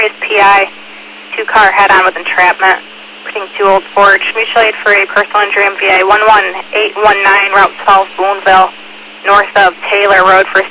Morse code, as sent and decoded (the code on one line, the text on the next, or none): none